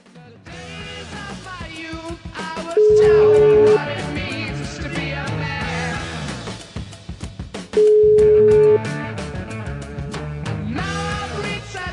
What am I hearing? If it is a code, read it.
Ringback tone (Europe, 425 Hz).